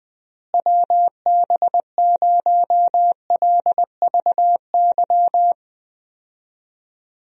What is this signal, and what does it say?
Morse code, heard as a keyed tone, about 21 wpm, 700 Hz. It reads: WB0LVY
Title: USA Call Signs - 6 Characters - Repeated 2x - 20wpm